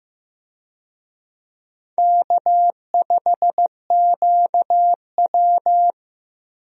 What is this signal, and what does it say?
Morse code, heard as a keyed tone, about 15 wpm, 700 Hz. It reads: K5QW
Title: USA Call Signs - 4 Characters - Repeated 3x - 15wpm